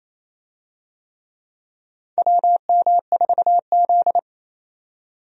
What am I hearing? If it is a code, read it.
Morse code, heard as a keyed tone, about 28 wpm, 700 Hz. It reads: WM4Z
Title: USA Call Signs - 4 Characters - 28wpm